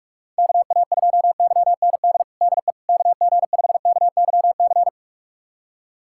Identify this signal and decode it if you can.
Morse code, heard as a keyed tone, about 43 wpm, 700 Hz. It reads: KA2YND DE KG5KYC
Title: USA Callsign DE Callsign - 6 Characters - 45wpm